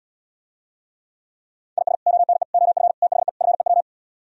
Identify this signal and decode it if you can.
Morse code, heard as a keyed tone, about 78 wpm, 700 Hz. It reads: IS ONE OF THE FEW